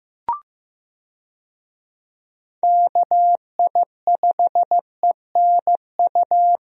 Morse code, heard as a keyed tone, about 16 wpm, 700 Hz. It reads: KI5ENU
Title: USA Call Signs - 6 Characters - Repeated 2x - 15wpm